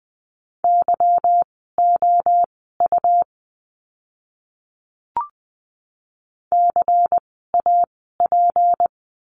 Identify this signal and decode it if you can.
Morse code, heard as a keyed tone, about 20 wpm, spaced as for 14 wpm, 700 Hz. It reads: YOU CAP
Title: 153 Lesson 25f - Letter p - Words 20wpm